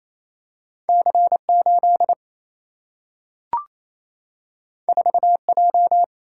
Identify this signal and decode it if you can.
Morse code, heard as a keyed tone, about 28 wpm, 700 Hz. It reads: C8 4J
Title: Single-Letter Single-Number 28wpm